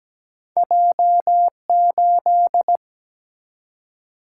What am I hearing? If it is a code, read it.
Morse code, heard as a keyed tone, about 17 wpm, 700 Hz. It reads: J8
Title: Single-Letter Single-Number 17wpm